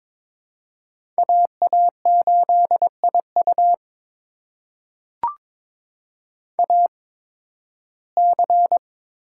Morse code, heard as a keyed tone, about 22 wpm, spaced as for 14 wpm, 700 Hz. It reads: AA8IU AC
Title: USA Call Signs - 5 Characters - 8x Character Spacing - 22wpm